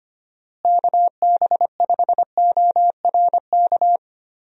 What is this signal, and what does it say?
Morse code, heard as a keyed tone, about 25 wpm, 700 Hz. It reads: KB5ORK
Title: USA Call Signs - 6 Characters - Repeated 2x - 25wpm